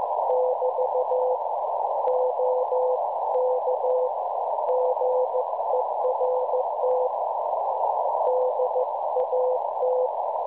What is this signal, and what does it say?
Morse code, heard as a keyed tone, about 15 wpm, 510 Hz, with noise. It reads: <BT> OK GERT DAT
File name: cwronde / websdr recording .6kHz